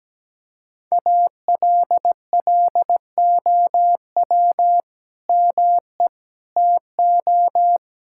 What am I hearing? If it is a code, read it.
Morse code, heard as a keyed tone, about 17 wpm, 700 Hz. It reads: ALLOW ME TO
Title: Sentences from Top 1600 Words 17wpm